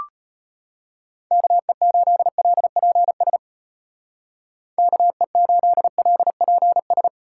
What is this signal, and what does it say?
Morse code, heard as a keyed tone, about 34 wpm, 700 Hz. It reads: KE8LPS KE8LPS